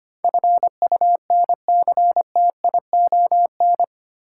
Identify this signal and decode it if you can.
Morse code, heard as a keyed tone, about 25 wpm, 690 Hz. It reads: FUNCTION
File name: 8 Letter Words - First 3 Letters Given - 25wpm